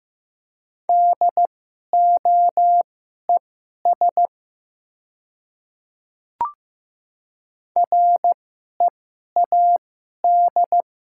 Morse code, heard as a keyed tone, about 15 wpm, spaced as for 11 wpm, 700 Hz. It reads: DOES READ
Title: 2x Character Spacing - Top 200 Words - 15wpm